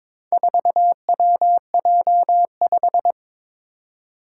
Morse code, heard as a keyed tone, about 22 wpm, 700 Hz. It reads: 4WJ5
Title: Four Challenging Letter-Number Combinations 22wpm